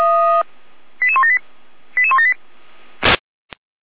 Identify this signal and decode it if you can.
Five-tone selective call (ZVEI), heard as quick runs of five tones, tones of about 80 ms. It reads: 88178 88178
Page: ZVEI